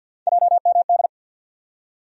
Morse code, heard as a keyed tone, about 49 wpm, 690 Hz. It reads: JMD
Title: USA Call Sign Postfixes - 3 Letters 50wpm